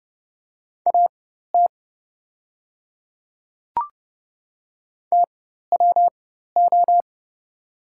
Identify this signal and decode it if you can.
Morse code, heard as a keyed tone, about 30 wpm, spaced as for 14 wpm, 700 Hz. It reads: AT TWO
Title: 4x Character Spacing - Top 100 Words - 30wpm